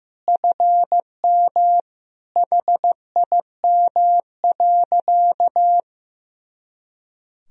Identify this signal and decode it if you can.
Morse code, heard as a keyed tone, about 15 wpm, 700 Hz. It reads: FM HIM.
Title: Sentences from Top 400 Words and Abbreviations 15wpm